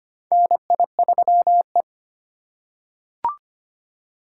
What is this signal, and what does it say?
Morse code, heard as a keyed tone, about 25 wpm, 700 Hz. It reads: NI3E
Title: USA Call Signs - 3x Character Spacing - 25wpm